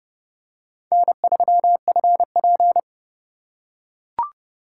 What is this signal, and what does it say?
Morse code, heard as a keyed tone, about 26 wpm, 700 Hz. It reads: N3FP